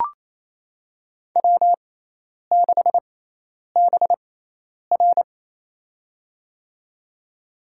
Morse code, heard as a keyed tone, about 28 wpm, spaced as for 10 wpm, 700 Hz. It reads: W6BR
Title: USA Call Signs - 4 Characters - 6x Character Spacing - 28wpm